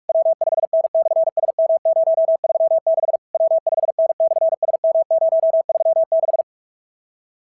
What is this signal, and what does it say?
Morse code, heard as a keyed tone, about 44 wpm, 650 Hz. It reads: W5N/SM036 W5N/SM036
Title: SOTA QSO Practice 45wpm